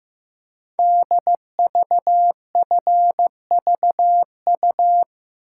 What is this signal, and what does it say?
Morse code, heard as a keyed tone, about 15 wpm, 700 Hz. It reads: DVFVU
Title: Code Groups - 5 Letters - 15wpm